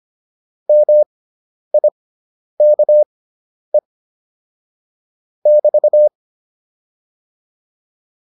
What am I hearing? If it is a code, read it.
Morse code, heard as a keyed tone, about 25 wpm, spaced as for 10 wpm, 600 Hz. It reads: MIKE <BT>
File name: QSO310 10